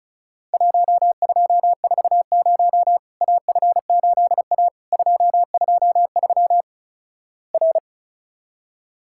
Morse code, heard as a keyed tone, about 36 wpm, 700 Hz. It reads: 1240 AF8A 223 R
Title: Worked All Europe DX Contest - QTC Exchange - 35wpm